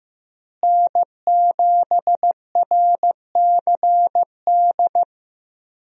Morse code, heard as a keyed tone, about 16 wpm, 700 Hz. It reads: N7RCD
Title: USA Call Signs - 5 Characters - Repeated 3x - 15wpm